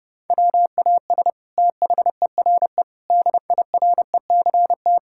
Morse code, heard as a keyed tone, about 30 wpm, 700 Hz. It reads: WAS THERE DIRECT
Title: Sentences from Top 1000 Words - Review Part 6 - 30wpm